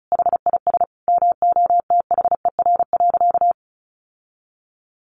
Morse code, heard as a keyed tone, about 34 wpm, 700 Hz. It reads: HIS MOTHER.
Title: Sentences from Top 1300 Words 35wpm